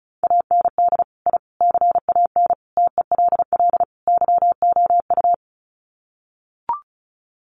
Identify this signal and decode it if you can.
Morse code, heard as a keyed tone, about 34 wpm, 700 Hz. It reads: AND I CAN TELL YOU
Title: Sets of 5 Words Encore2 35wpm